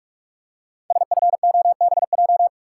Morse code, heard as a keyed tone, about 44 wpm, 700 Hz. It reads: SFOBJ
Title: Code Groups - 5 Letters - 45wpm